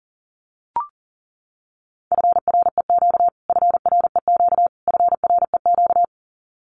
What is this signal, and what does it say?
Morse code, heard as a keyed tone, about 40 wpm, 700 Hz. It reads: FREQ FREQ FREQ
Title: Abbreviations Repeated 3x - 40wpm